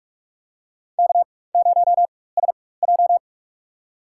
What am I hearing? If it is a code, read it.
Morse code, heard as a keyed tone, about 46 wpm, spaced as for 21 wpm, 700 Hz. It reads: K0SJ
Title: USA Call Signs - 4 Characters - 4x Character Spacing - 45wpm